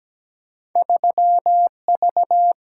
Morse code, heard as a keyed tone, about 17 wpm, 700 Hz. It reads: 3V
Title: Single-Letter Single-Number 17wpm